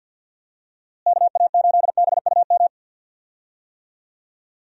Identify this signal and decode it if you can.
Morse code, heard as a keyed tone, about 51 wpm, 700 Hz. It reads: KA8BUM
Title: USA Call Signs - 6 Characters - Repeated 3x - 50wpm